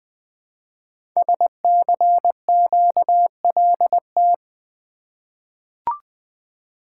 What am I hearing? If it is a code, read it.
Morse code, heard as a keyed tone, about 20 wpm, 700 Hz. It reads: SCQLT
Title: Code Groups - 5 Letters - 20wpm